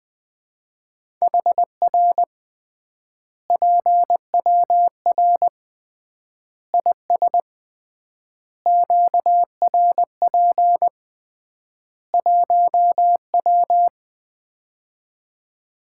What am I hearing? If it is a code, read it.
Morse code, heard as a keyed tone, about 20 wpm, 700 Hz. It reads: HR PWR IS QRP 1W
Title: Common QSO Phrases - 3x Word-Spacing - 20wpm